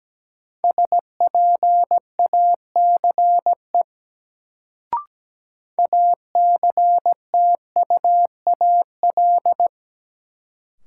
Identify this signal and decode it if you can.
Morse code, heard as a keyed tone, about 17 wpm, 700 Hz. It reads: SPACE ACTUAL
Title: Sentences from Top 1000 Words - Review Part 5 - New Word Drill - 17wpm